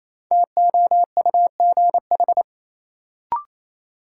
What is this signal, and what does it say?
Morse code, heard as a keyed tone, about 28 wpm, 700 Hz. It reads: TOUGH
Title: Sentences from Top 1200 Words - New Word Drill - 28wpm